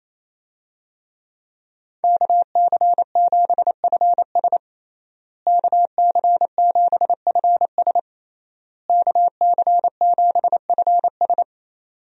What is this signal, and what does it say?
Morse code, heard as a keyed tone, about 28 wpm, 700 Hz. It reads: KC7FS KC7FS KC7FS
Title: USA Call Signs - 5 Characters - Repeated 3x - 28wpm